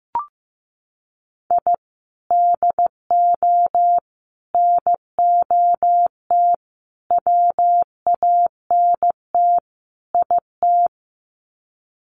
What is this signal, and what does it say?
Morse code, heard as a keyed tone, about 15 wpm, 700 Hz. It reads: I DO NOT WANT IT
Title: Sets of 5 Words Encore2 15wpm